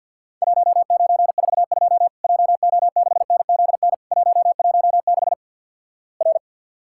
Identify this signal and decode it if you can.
Morse code, heard as a keyed tone, about 50 wpm, 700 Hz. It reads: 1942 JO6NZN 116 R
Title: Worked All Europe DX Contest - QTC Exchange - 50wpm